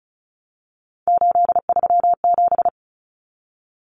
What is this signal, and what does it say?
Morse code, heard as a keyed tone, about 34 wpm, 700 Hz. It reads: 837
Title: Numbers - 3-Digit 35wpm